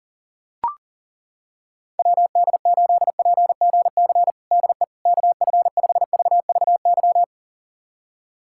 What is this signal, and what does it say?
Morse code, heard as a keyed tone, about 40 wpm, 700 Hz. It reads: WD8PGC DE KF5VVY
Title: USA Callsign DE Callsign - 6 Characters - 40wpm